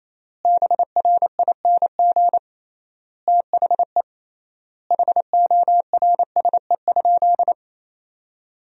Morse code, heard as a keyed tone, about 28 wpm, 700 Hz. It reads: BRING THE HORSE?
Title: 3x Word-Spacing - Sentences from Top 500 Words - 28wpm